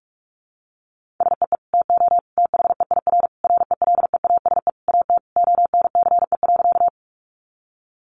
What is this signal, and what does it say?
Morse code, heard as a keyed tone, about 44 wpm, 700 Hz. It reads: SEE TO THEIR RELEASE AT ONCE.
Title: Sentences from Top 1100 Words 45wpm